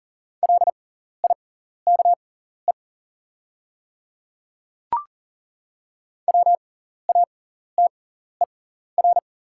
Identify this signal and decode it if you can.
Morse code, heard as a keyed tone, about 40 wpm, spaced as for 14 wpm, 700 Hz. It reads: LIKE WATER
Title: 6x Character Spacing - Top 500 Words - 40wpm